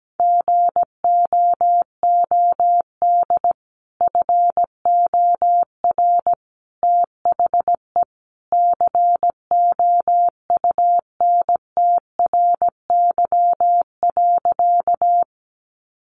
Morse code, heard as a keyed tone, about 17 wpm, 700 Hz. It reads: GOOD FOR THE COUNTRY.
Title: Sentences from Top 300 Words 17wpm